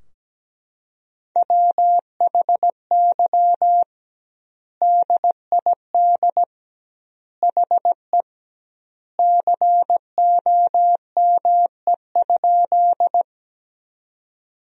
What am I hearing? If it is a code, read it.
Morse code, heard as a keyed tone, about 17 wpm, 700 Hz. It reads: WHY DID HE COME?
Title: 2x Word-Spacing - Sentences from Top 200 Words - 17wpm